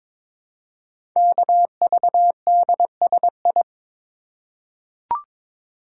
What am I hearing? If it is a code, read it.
Morse code, heard as a keyed tone, about 22 wpm, 700 Hz. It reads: KVDSI
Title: Code Groups - 5 Letters - 22wpm